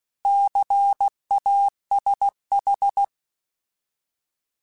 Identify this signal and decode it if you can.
Morse code, heard as a keyed tone, about 16 wpm, 800 Hz. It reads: CASH
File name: Alphabetical / Alphabetical mp3s / Cs @15